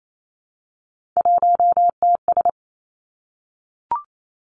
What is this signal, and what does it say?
Morse code, heard as a keyed tone, about 28 wpm, 700 Hz. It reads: 1TS
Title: Three Challenging Letter-Number Combinations 28wpm